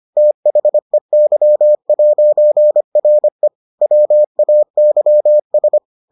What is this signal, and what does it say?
Morse code, heard as a keyed tone, about 25 wpm, 600 Hz. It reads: THEY'RE WAYS